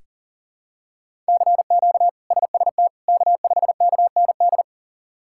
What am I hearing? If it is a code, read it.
Morse code, heard as a keyed tone, about 40 wpm, 690 Hz. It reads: CQ SST K5KND
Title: K1USN SST QSO Practice - 40wpm